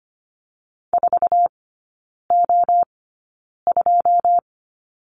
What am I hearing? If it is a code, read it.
Morse code, heard as a keyed tone, about 25 wpm, spaced as for 9 wpm, 700 Hz. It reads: 4O2
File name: Lesson 14e - Number 2 - Three Characters 25wpm